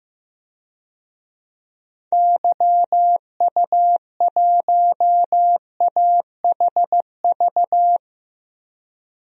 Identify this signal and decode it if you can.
Morse code, heard as a keyed tone, about 16 wpm, 700 Hz. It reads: YU1AHV
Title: DXCC-100 Call Signs - 6 Characters - 15wpm